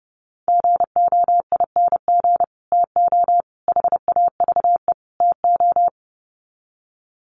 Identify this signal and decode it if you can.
Morse code, heard as a keyed tone, about 30 wpm, 700 Hz. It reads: GOING TO HAVE TO
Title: Sets of 4 Words 30wpm